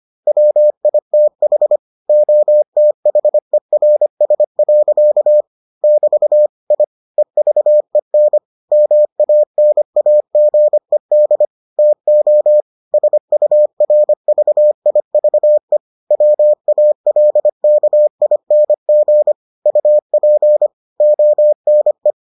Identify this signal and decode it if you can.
Morse code, heard as a keyed tone, about 25 wpm, 600 Hz. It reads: WITH OTHERS. <BT> I EVEN MANAGED TO SURVIVE WALKING UP ONE